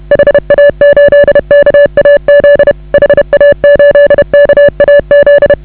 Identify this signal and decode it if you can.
Morse code, heard as a keyed tone, about 30 wpm, 560 Hz. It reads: HA8KAZ HA8KAZ
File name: AUDIO2019 / HA8KAZ CW